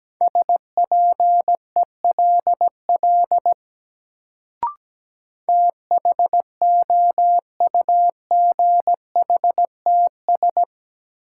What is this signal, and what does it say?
Morse code, heard as a keyed tone, about 17 wpm, 700 Hz. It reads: SPELL THOUGHTS
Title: Sentences from Top 1000 Words - Review Part 1 - New Word Drill - 17wpm